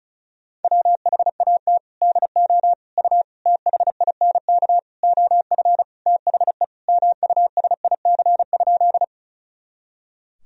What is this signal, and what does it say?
Morse code, heard as a keyed tone, about 36 wpm, 700 Hz. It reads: WHAT DO U THINK OF THE MUSIC?
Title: Sentences from Top 300 Words and Abbreviations 35wpm